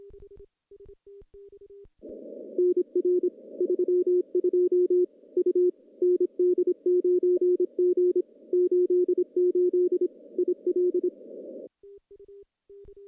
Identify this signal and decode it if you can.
Morse code, heard as a keyed tone, about 26 wpm, 360 Hz. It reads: NR 32 U ND9G 88 IL
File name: contests / CW-SS-k5tr- -1400